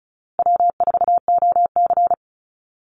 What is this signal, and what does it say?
Morse code, heard as a keyed tone, about 34 wpm, 700 Hz. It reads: W4OC
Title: Easy Call Signs Repeated 3x - 35wpm